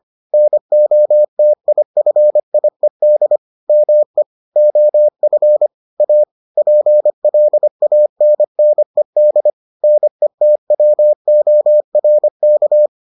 Morse code, heard as a keyed tone, about 25 wpm, 600 Hz. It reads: NOTIFIED ME OF A PLANNED NETWORK